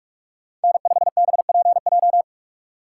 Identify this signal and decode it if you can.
Morse code, heard as a keyed tone, about 46 wpm, 690 Hz. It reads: N5BPJ
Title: USA Call Signs - 4x Character Spacing - 45wpm